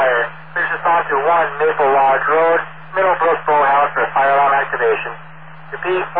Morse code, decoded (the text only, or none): none